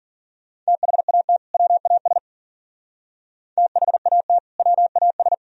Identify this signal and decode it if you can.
Morse code, heard as a keyed tone, about 52 wpm, 700 Hz. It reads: THAT WAS THAT WAS